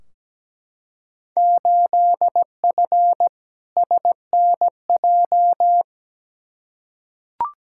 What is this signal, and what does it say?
Morse code, heard as a keyed tone, about 17 wpm, 700 Hz. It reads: 8F SNJ
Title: Field Day Exchanges Encore - 17wpm